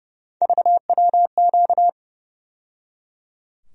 Morse code, heard as a keyed tone, about 32 wpm, 700 Hz. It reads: VWQ
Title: USA Call Sign Postfixes - 3 Letters 30wpm